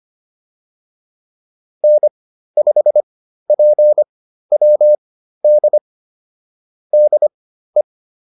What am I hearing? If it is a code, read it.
Morse code, heard as a keyed tone, about 25 wpm, spaced as for 13 wpm, 600 Hz. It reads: N5PWD DE
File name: QSO208 13